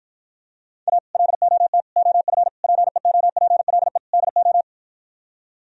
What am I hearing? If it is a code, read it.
Morse code, heard as a keyed tone, about 53 wpm, 700 Hz. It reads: A LOT OF PEOPLE DO